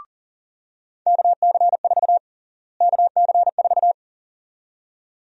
Morse code, heard as a keyed tone, about 40 wpm, 700 Hz. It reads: KC4 KC4
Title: USA Call Sign Prefixes - Repeated 2x - 40wpm